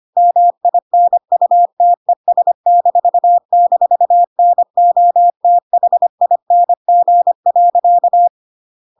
Morse code, heard as a keyed tone, about 25 wpm, 700 Hz. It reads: MINUTES--NOTHING.
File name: Pinocchio / Pinocchio ch1 25 25 ik1hge